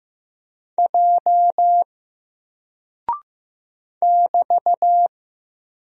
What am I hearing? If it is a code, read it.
Morse code, heard as a keyed tone, about 15 wpm, 700 Hz. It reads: J <BT>
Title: Single Character (Letter:Number:Punctuation:Prosign) Alt 15wpm